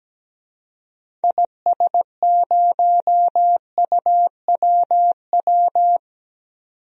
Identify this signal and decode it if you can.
Morse code, heard as a keyed tone, about 17 wpm, 700 Hz. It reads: IS0UWW
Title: DXCC-100 Call Signs - 6 Characters - 17wpm